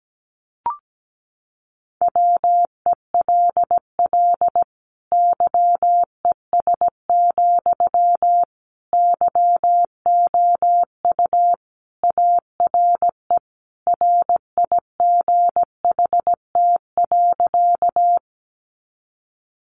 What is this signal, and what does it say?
Morse code, heard as a keyed tone, about 17 wpm, 700 Hz. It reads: WELL YES, YOU ARE RIGHT.
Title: Sentences from Top 500 Words 17wpm